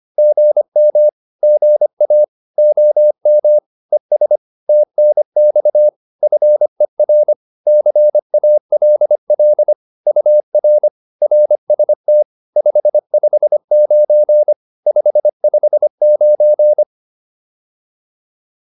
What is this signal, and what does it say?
Morse code, heard as a keyed tone, about 25 wpm, 600 Hz. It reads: GM GA OM ES TNX FER CALL UR RST 559 559
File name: QSO 410 25